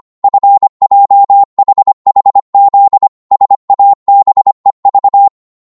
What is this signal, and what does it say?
Morse code, heard as a keyed tone, about 25 wpm, 800 Hz. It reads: FJHHZ SABEV